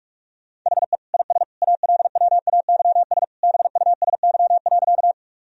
Morse code, heard as a keyed tone, about 41 wpm, 700 Hz. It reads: HE IS ALWAYS BUSY.
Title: Sentences from Top 1000 Words - Review Part 3 - 45wpm